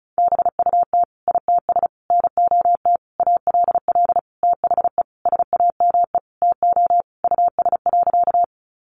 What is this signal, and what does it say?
Morse code, heard as a keyed tone, about 34 wpm, 700 Hz. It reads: BUT ITS NOT ALL THE SAME TO US.
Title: Sentences from Top 600 Words 35wpm